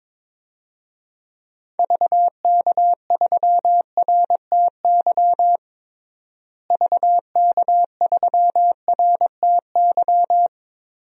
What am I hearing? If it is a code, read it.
Morse code, heard as a keyed tone, about 22 wpm, 700 Hz. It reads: VK3RTY VK3RTY
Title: DXCC-100 Call Signs - 6 Characters - Repeated 3x - 22wpm